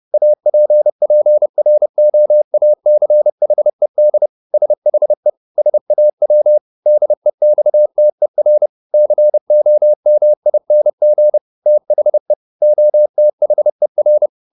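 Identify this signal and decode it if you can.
Morse code, heard as a keyed tone, about 30 wpm, 600 Hz. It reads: APPROACHED SHE SAW DEXTER COMING THE OTHER
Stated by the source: Morse code at 30 wpm, the novel Never by Ken Follett in English, from CW med Gnister.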